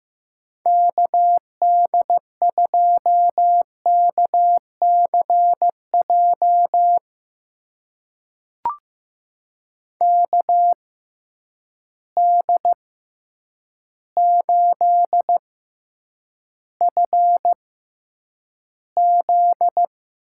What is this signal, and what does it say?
Morse code, heard as a keyed tone, about 15 wpm, spaced as for 8 wpm, 700 Hz. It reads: KD2KCJ KD8FZ
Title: USA Call Signs - 6 Characters - 6x Character Spacing - 15wpm